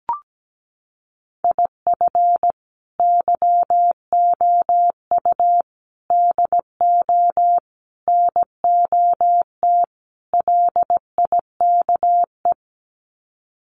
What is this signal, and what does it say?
Morse code, heard as a keyed tone, about 17 wpm, 700 Hz. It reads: IF YOU DO NOT LIKE
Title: Sets of 5 Words 17wpm